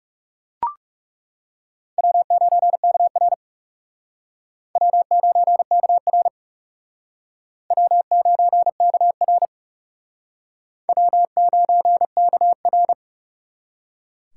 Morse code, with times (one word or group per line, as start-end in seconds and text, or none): 2.0-3.3 W9KR
4.8-6.3 W9KR
7.7-9.4 W9KR
10.9-12.9 W9KR